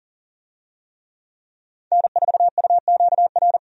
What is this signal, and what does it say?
Morse code, heard as a keyed tone, about 40 wpm, 700 Hz. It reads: N4UQR
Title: USA Call Signs - 5 Characters - 40wpm